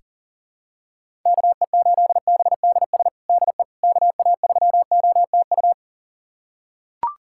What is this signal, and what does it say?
Morse code, heard as a keyed tone, about 40 wpm, 700 Hz. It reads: KE8BDS DE KA3OTU
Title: USA Callsign DE Callsign 40wpm